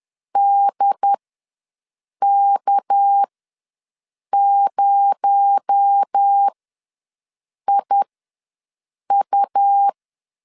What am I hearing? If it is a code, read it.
Morse code, heard as a keyed tone, about 10 wpm, spaced as for 6 wpm, 800 Hz. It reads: DK0IU
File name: LEZIONI / 6 PKX0 / 6 PKX0 ws 10wpm 4220